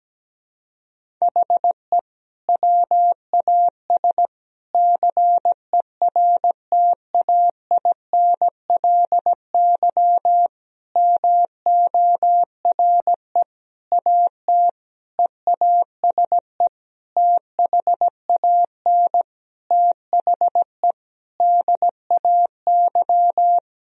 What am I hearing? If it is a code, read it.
Morse code, heard as a keyed tone, about 17 wpm, 700 Hz. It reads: HE WAS CERTAINLY MORE AT EASE THAN THE DAY
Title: Sentences from Top 1000 Words - Review Part 1 - 17wpm